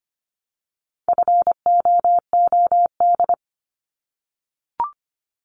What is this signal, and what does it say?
Morse code, heard as a keyed tone, about 25 wpm, 700 Hz. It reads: FOOD
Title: Multi-letter Phonemes 25wpm